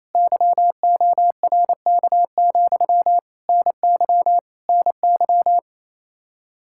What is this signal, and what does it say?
Morse code, heard as a keyed tone, about 28 wpm, 700 Hz. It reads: YORK, NY NY
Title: Practice QSO - Exchange 1 - Beginning - 28wpm